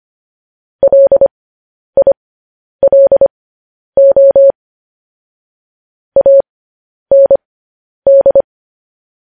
Morse code, heard as a keyed tone, about 25 wpm, spaced as for 10 wpm, 550 Hz. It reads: LILO AND
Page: news-headlines-in-morse-code-at-10-wpm